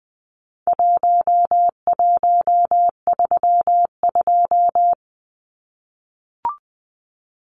Morse code, heard as a keyed tone, about 20 wpm, 700 Hz. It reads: 1132
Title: Numbers - Contest Style - 4-Digit 20wpm